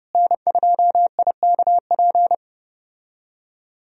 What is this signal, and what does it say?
Morse code, heard as a keyed tone, about 31 wpm, 700 Hz. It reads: N2IKP
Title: USA Call Signs - 5 Characters - Repeated 3x - 30wpm